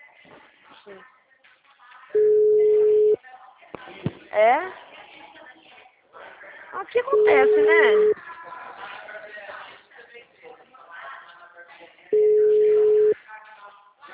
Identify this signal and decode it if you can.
Ringback tone (Europe, 425 Hz), with speech.